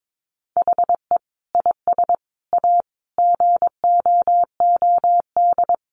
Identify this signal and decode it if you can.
Morse code, heard as a keyed tone, about 22 wpm, 700 Hz. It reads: HE IS A GOOD
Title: Sentences from Top 1100 Words 22wpm